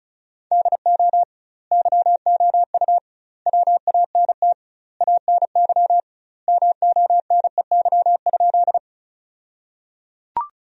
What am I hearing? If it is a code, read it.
Morse code, heard as a keyed tone, about 34 wpm, 700 Hz. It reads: DO YOU WANT ANY MONEY?
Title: 2x Word-Spacing - Sentences from Top 400 Words - 35wpm